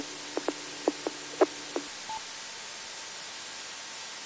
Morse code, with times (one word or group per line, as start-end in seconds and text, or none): none